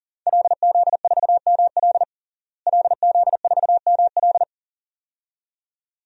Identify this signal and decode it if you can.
Morse code, heard as a keyed tone, about 37 wpm, 700 Hz. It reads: LZ4ML LZ4ML
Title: DXCC-100 Call Signs - 5 Characters - Repeated 2x - 40wpm